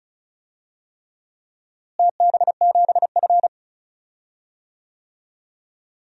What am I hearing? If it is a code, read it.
Morse code, heard as a keyed tone, about 36 wpm, 700 Hz. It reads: TB7F
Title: DXCC-100 Call Signs - 4 Characters - Repeated 3x - 35wpm